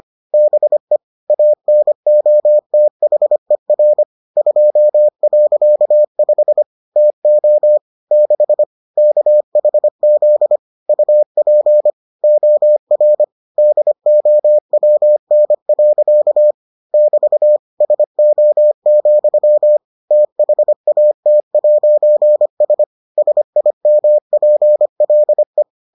Morse code, heard as a keyed tone, about 25 wpm, 600 Hz. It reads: BE ANOTHER 2.5 TO 6 KHZ UP OR DOWN. <BT> SO, THAT'S SIMPLE